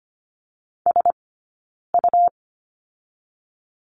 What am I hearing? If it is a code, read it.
Morse code, heard as a keyed tone, about 25 wpm, 700 Hz. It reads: S U
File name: Lesson 17d - Letter u - Two Characters 25wpm